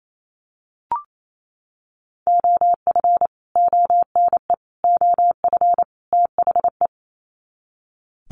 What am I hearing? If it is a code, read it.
Morse code, heard as a keyed tone, about 28 wpm, 700 Hz. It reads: OF ONE OF THE
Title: Sets of 4 Words 28wpm